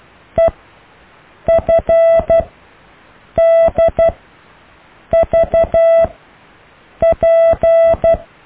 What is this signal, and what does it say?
Morse code, heard as a keyed tone, about 12 wpm, spaced as for 6 wpm, 660 Hz. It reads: EFDVP
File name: cw / EKTPV+BFHNW